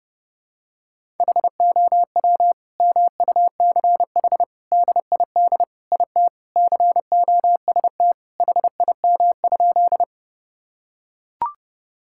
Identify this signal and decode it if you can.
Morse code, heard as a keyed tone, about 30 wpm, 700 Hz. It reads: HOW MUCH DID IT COST HIM?
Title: Sentences from Top 1000 Words - Review Part 5 - 30wpm